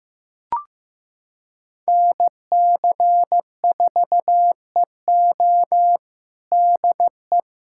Morse code, heard as a keyed tone, about 15 wpm, 700 Hz. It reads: NC4EO DE
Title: USA Callsign DE Callsign 15wpm